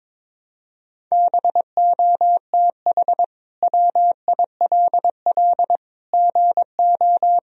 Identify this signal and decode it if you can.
Morse code, heard as a keyed tone, about 21 wpm, 700 Hz. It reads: BOTH WILL GO
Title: Sentences from Top 1000 Words - Review Part 2 - 22wpm